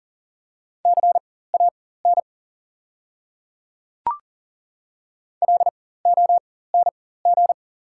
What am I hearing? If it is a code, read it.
Morse code, heard as a keyed tone, about 40 wpm, spaced as for 19 wpm, 690 Hz. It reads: CAN LONG
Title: 4x Character Spacing - Top 100 Words - 40wpm